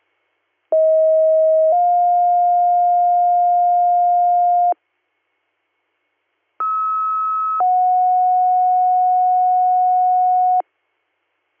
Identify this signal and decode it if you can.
Two-tone page, heard as a one-second tone, then a three-second tone, about 640 then 710 Hz, about 1290 then 730 Hz. It is heard from 0.7 to 10.6 s.